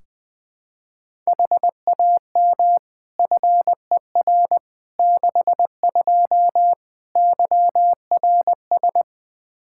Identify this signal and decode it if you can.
Morse code, heard as a keyed tone, about 20 wpm, 700 Hz. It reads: HAM FER 62 YRS
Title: Practice QSO - Exchange 3 - More Info - 20wpm